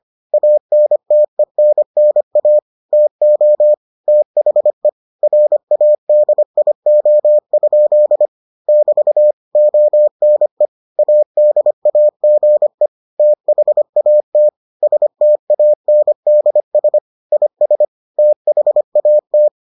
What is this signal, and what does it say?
Morse code, heard as a keyed tone, about 25 wpm, 600 Hz. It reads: ANTENNA TO THE RADIO? <BT> ONE ADAGE THAT STANDS IS THAT